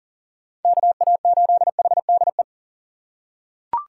Morse code, heard as a keyed tone, about 37 wpm, 700 Hz. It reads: KA8HDE
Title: K1USN SST QSO Practice - 40wpm